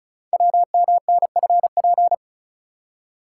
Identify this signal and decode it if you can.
Morse code, heard as a keyed tone, about 36 wpm, 690 Hz. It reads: WMNFP
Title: Code Groups - 5 Letters - 35wpm